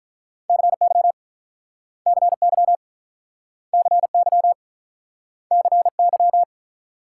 Morse code, heard as a keyed tone, about 48 wpm, 700 Hz. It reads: CY CY CY CY